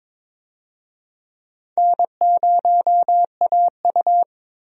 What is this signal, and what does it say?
Morse code, heard as a keyed tone, about 22 wpm, 700 Hz. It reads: N0AU
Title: USA Call Signs - 4 Characters - Repeated 3x - 22wpm